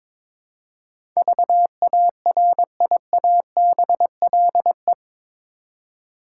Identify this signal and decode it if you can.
Morse code, heard as a keyed tone, about 21 wpm, 700 Hz. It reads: VARIABLE
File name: practice / 22wpm / Eight Letter Words - 22wpm